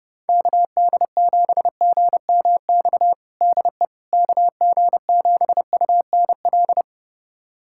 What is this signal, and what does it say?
Morse code, heard as a keyed tone, about 30 wpm, 700 Hz. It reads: KD7GMX DE KG7UNL
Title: USA Callsign DE Callsign - 6 Characters - 30wpm